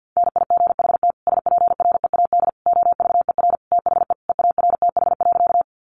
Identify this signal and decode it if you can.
Morse code, heard as a keyed tone, about 50 wpm, 700 Hz. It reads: NIGHT SPREAD OVER THE EARTH.
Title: Sentences from Top 1200 Words 50wpm